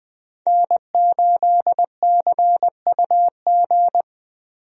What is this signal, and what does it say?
Morse code, heard as a keyed tone, about 21 wpm, 700 Hz. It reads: N8CUG
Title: USA Call Signs - 5 Characters - 20wpm